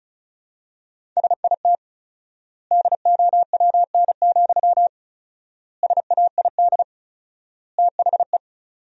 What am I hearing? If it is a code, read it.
Morse code, heard as a keyed tone, about 36 wpm, 700 Hz. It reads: SIT DOWN, SAID THE
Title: 4x Word-Spacing - Sentences from Top 500 Words - 35wpm